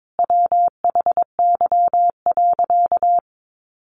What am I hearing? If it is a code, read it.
Morse code, heard as a keyed tone, about 22 wpm, 700 Hz. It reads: WHY.
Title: Sentences from Top 1300 Words 22wpm